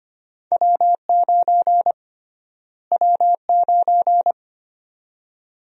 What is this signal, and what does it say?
Morse code, heard as a keyed tone, about 25 wpm, 700 Hz. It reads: W9 W9
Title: USA Call Sign Prefixes - Repeated 2x - 25wpm